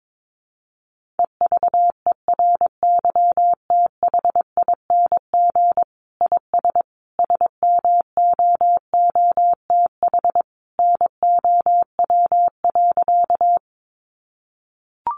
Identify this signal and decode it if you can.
Morse code, heard as a keyed tone, about 22 wpm, 700 Hz. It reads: EVERYTHING IS SMOOTH NOW.
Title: Sentences from Top 1400 Words 22wpm